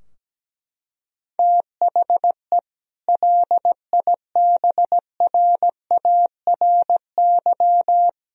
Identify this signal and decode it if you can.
Morse code, heard as a keyed tone, about 17 wpm, 700 Hz. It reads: THE LIBRARY
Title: Sentences from Top 1000 Words - Review Part 4 - 17wpm